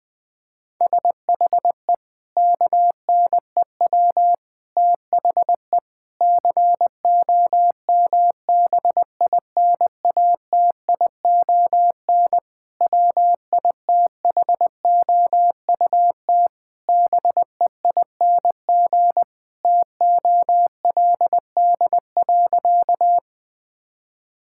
Morse code, heard as a keyed tone, about 20 wpm, 700 Hz. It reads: SHE KNEW THE COMBINATION WITHOUT BEING TOLD.
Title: Sentences from Top 1000 Words - Review Part 6 - 20wpm